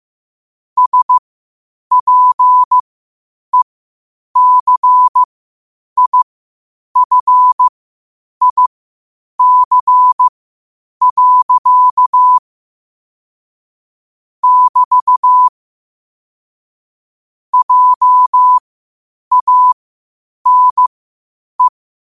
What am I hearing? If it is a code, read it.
Morse code, heard as a keyed tone, about 15 wpm, spaced as for 8 wpm, 1000 Hz. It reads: SPECIFIC. <BT> JANE
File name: qotdfiles / 8wpm